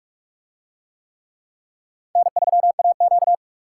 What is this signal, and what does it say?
Morse code, heard as a keyed tone, about 46 wpm, 700 Hz. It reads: N3AQ